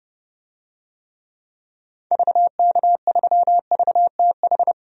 Morse code, heard as a keyed tone, about 30 wpm, 700 Hz. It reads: VK3VTH
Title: DXCC-100 Call Signs - 6 Characters - 30wpm